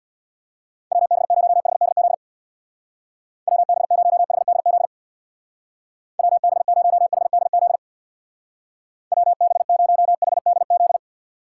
Morse code, heard as a keyed tone, about 63 wpm, 700 Hz. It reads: WB0HDZ WB0HDZ WB0HDZ WB0HDZ